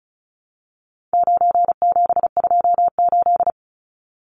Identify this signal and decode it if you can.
Morse code, heard as a keyed tone, about 34 wpm, 700 Hz. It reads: 9728
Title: Numbers - 4-Digit 35wpm